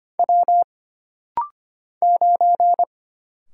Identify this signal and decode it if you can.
Morse code, heard as a keyed tone, about 25 wpm, 700 Hz. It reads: W 9
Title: Single Letter-Number - Rapid-Fire 25wpm